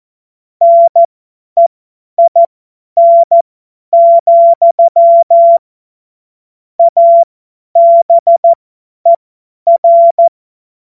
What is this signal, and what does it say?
Morse code, heard as a keyed tone, about 14 wpm, spaced as for 10 wpm, 680 Hz. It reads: NEIN, ABER
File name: Morsekurs / Witze / Witz 06. Januar 2026 50 F70